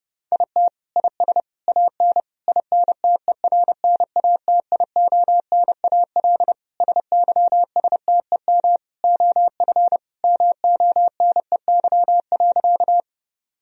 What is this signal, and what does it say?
Morse code, heard as a keyed tone, about 30 wpm, 700 Hz. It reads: IT IS AN INTERNATIONAL SYSTEM OF MONEY.
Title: Sentences from Top 1000 Words - Review Part 4 - 30wpm